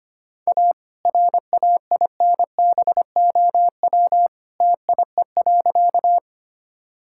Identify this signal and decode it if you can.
Morse code, heard as a keyed tone, about 25 wpm, 700 Hz. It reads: A RAINBOW TIE.
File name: practice / 25wpm / All-Age Appropriate Jokes - 25wpm